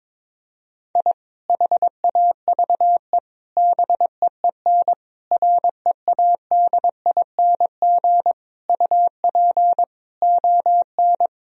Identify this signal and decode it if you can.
Morse code, heard as a keyed tone, about 21 wpm, 700 Hz. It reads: I HAVE BEEN READING UP ON
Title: Sentences from Top 1000 Words - Review Part 3 - 22wpm